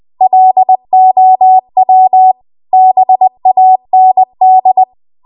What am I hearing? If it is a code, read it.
Morse code, heard as a keyed tone, about 20 wpm, 750 Hz. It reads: LOW BAND
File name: Archive / 20 WPM / 20WPM